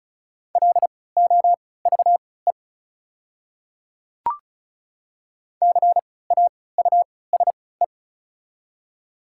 Morse code, heard as a keyed tone, about 36 wpm, spaced as for 20 wpm, 690 Hz. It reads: LOVE CAUSE
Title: 3x Character Spacing - Top 500 Words - 35wpm